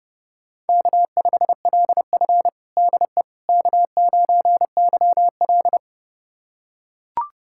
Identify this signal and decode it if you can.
Morse code, heard as a keyed tone, about 32 wpm, 700 Hz. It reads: K5LF DE K9YL
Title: USA Callsign DE Callsign - 4 Characters - 30wpm